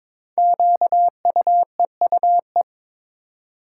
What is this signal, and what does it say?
Morse code, heard as a keyed tone, about 22 wpm, 700 Hz. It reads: QUEUE
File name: 5 Letter Words - First 3 Letters Given - 22wpm